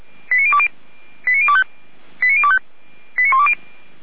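Five-tone selective call (ZVEI), heard as quick runs of five tones, tones of about 70 ms. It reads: 89020 89025 89025 89120
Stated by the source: ZVEI five-tone selective call 89270, received by SDR.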